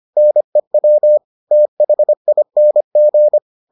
Morse code, heard as a keyed tone, about 25 wpm, 600 Hz. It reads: NEW THING